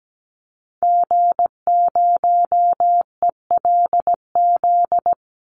Morse code, heard as a keyed tone, about 17 wpm, 700 Hz. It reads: G0ELZ
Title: Call Signs Repeated 3x - 17wpm